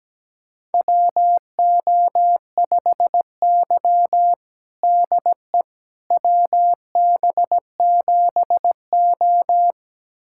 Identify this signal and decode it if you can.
Morse code, heard as a keyed tone, about 17 wpm, 700 Hz. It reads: WO5Y DE WB7O
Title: USA Callsign DE Callsign - 4 Characters - 17wpm